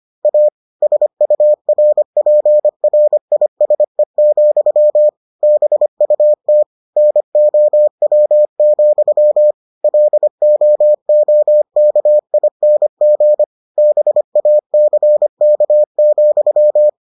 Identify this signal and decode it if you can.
Morse code, heard as a keyed tone, about 25 wpm, 600 Hz. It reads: A SURPRISE, BUT NOW, LOOKING BACK,